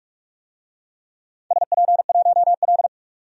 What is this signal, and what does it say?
Morse code, heard as a keyed tone, about 44 wpm, 700 Hz. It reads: SP1L
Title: DXCC-100 Call Signs - 4 Characters - 45wpm